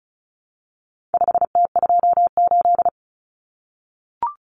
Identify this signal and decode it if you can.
Morse code, heard as a keyed tone, about 34 wpm, 700 Hz. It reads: 5T28
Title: Common Cut Numbers - 4-Digit 35wpm